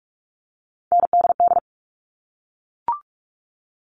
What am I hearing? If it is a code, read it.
Morse code, heard as a keyed tone, about 44 wpm, 700 Hz. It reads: NDD
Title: All Cut Numbers - 3-Digit 45wpm